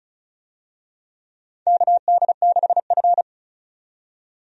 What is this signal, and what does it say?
Morse code, heard as a keyed tone, about 34 wpm, 700 Hz. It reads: KD6F
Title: USA Call Signs - 4 Characters - 35wpm